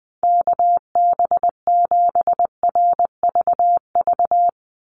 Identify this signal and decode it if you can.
Morse code, heard as a keyed tone, about 19 wpm, 700 Hz. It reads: KB7RVV
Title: USA Call Signs 20wpm